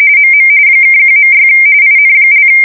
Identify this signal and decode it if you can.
RTTY, about 45.45 baud. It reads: MM (1.6 IN)